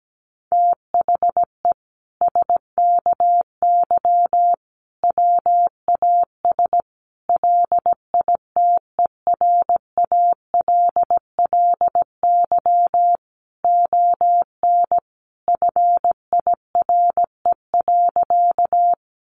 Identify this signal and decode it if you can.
Morse code, heard as a keyed tone, about 17 wpm, 700 Hz. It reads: THE SKY WAS LITERALLY ON FIRE.
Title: Sentences from Top 1400 Words 17wpm